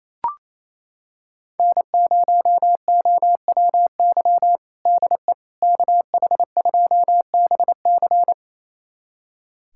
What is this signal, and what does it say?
Morse code, heard as a keyed tone, about 28 wpm, 700 Hz. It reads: N0OWY DE KH2BC
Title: USA Callsign DE Callsign - 5 Characters - 28wpm